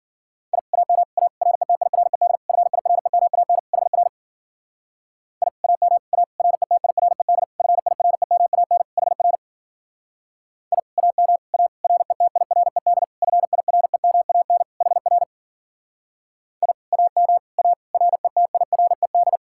Morse code, heard as a keyed tone, about 50 wpm, 700 Hz. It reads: I AM A RETIRED FIREMAN HR I AM A RETIRED FIREMAN HR I AM A RETIRED FIREMAN HR I AM A RETIRED
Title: Speed-Racing Common QSO Phrases Encore - 40wpm